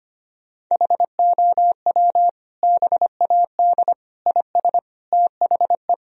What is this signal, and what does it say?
Morse code, heard as a keyed tone, about 25 wpm, 700 Hz. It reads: HOW BAD IS THE
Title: Sentences from Top 1000 Words - Review Part 3 - 25wpm